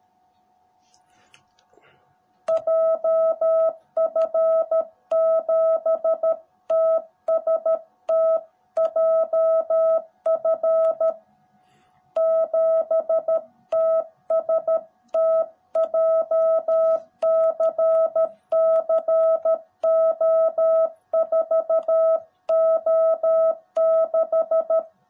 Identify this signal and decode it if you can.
Morse code, heard as a keyed tone, about 13 wpm, 650 Hz. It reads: JF7TST JF7TST JCCO4O6
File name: jh8cbh / 2021cw / cw18b